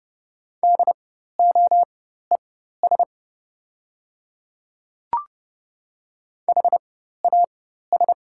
Morse code, heard as a keyed tone, about 30 wpm, spaced as for 14 wpm, 700 Hz. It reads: DOES HAS
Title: 4x Character Spacing - Top 200 Words - 30wpm